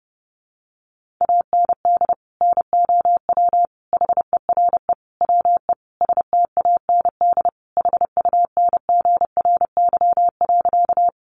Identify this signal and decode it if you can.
Morse code, heard as a keyed tone, about 28 wpm, 700 Hz. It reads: AND NOW HERE WE STAND HUNGRY.
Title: Sentences from Top 1400 Words 30wpm